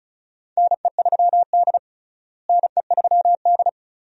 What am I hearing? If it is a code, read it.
Morse code, heard as a keyed tone, about 36 wpm, 700 Hz. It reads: NE3D NE3D
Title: USA Call Signs - 4 Characters - Repeated 2x - 35wpm